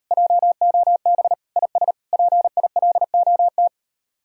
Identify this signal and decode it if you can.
Morse code, heard as a keyed tone, about 38 wpm, 700 Hz. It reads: JOB IS PILOT